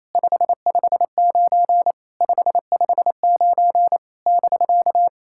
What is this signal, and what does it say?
Morse code, heard as a keyed tone, about 29 wpm, 700 Hz. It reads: 559 559 <BK>
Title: SOTA QSO Practice 28wpm